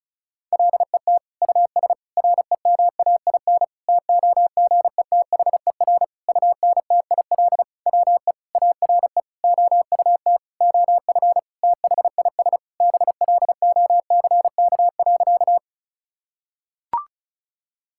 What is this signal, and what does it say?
Morse code, heard as a keyed tone, about 34 wpm, 700 Hz. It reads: LET US REMAIN TOGETHER UNTIL WE ARE OUT OF THIS BLOCK.
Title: Sentences from Top 1600 Words 35wpm